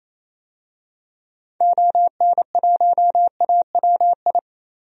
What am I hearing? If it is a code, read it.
Morse code, heard as a keyed tone, about 28 wpm, 700 Hz. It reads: ON1AWI
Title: DXCC-100 Call Signs - 6 Characters - 28wpm